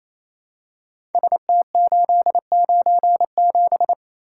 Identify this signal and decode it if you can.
Morse code, heard as a keyed tone, about 27 wpm, 700 Hz. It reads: ST897
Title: Code Groups - 5 Letter:Numbers - 28wpm